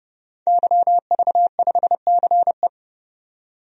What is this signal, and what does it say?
Morse code, heard as a keyed tone, about 30 wpm, 700 Hz. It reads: YV5CE
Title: DXCC-100 Call Signs - 5 Characters - 30wpm